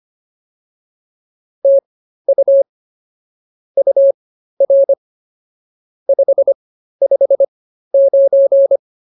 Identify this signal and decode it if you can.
Morse code, heard as a keyed tone, about 24 wpm, spaced as for 13 wpm, 550 Hz. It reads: TU UR 559